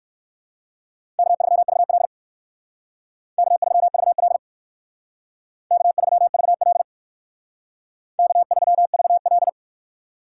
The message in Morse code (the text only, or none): K3VL K3VL K3VL K3VL